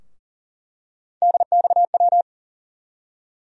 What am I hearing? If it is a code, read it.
Morse code, heard as a keyed tone, about 40 wpm, 700 Hz. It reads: DXW
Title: USA Call Sign Postfixes - 3 Letters 40wpm